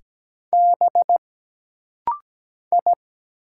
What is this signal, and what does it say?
Morse code, heard as a keyed tone, about 17 wpm, 700 Hz. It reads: B I
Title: Single Letters - Rapid-Fire 17wpm